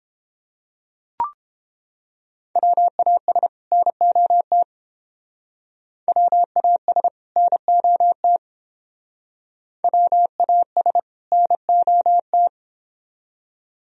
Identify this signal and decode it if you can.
Morse code, heard as a keyed tone, about 28 wpm, 700 Hz. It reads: WAS NOT WAS NOT WAS NOT